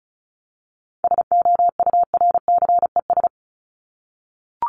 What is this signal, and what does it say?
Morse code, heard as a keyed tone, about 34 wpm, 700 Hz. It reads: SOURCES
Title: ES:S Suffix - Top 100 Words with Base Words 35wpm